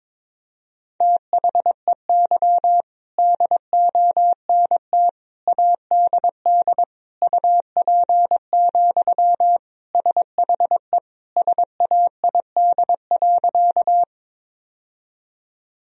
Morse code, heard as a keyed tone, about 22 wpm, 700 Hz. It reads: THEY DONT ADD UP, SHE SAID.
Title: Sentences from Top 1000 Words - Review Part 1 - 22wpm